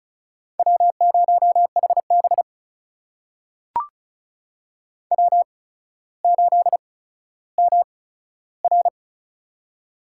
Morse code, heard as a keyed tone, about 36 wpm, spaced as for 15 wpm, 700 Hz. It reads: W0HB W8MR
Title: USA Call Signs - 4 Characters - 8x Character Spacing - 35wpm